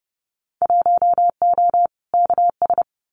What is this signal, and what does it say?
Morse code, heard as a keyed tone, about 30 wpm, 700 Hz. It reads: 1O KS
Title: Winter Field Day Exchanges 30wpm